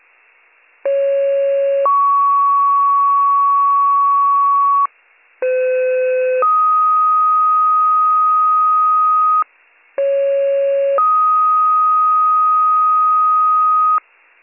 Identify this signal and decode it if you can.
Two-tone page, heard as a one-second tone, then a three-second tone, about 550 then 1120 Hz, about 510 then 1250 Hz, about 550 then 1250 Hz. It is heard from 0.8 to 14.0 s.